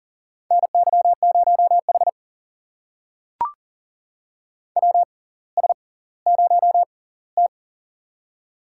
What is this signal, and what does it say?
Morse code, heard as a keyed tone, about 40 wpm, spaced as for 21 wpm, 700 Hz. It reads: NY0H WS0T
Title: USA Call Signs - 4 Characters - 6x Character Spacing - 40wpm